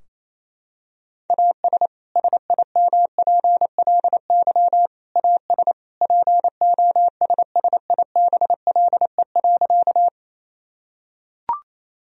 Morse code, heard as a keyed tone, about 28 wpm, 700 Hz. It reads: AS SIMPLY AS POSSIBLE.
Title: Sentences from Top 1000 Words - Review Part 4 - 28wpm